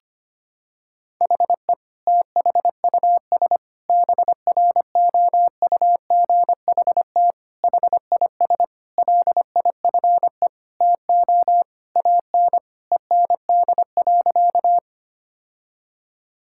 Morse code, heard as a keyed tone, about 25 wpm, 700 Hz. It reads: HE THUS BROUGHT HIS LIFE TO AN END.
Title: Sentences from Top 1000 Words - Review Part 5 - 25wpm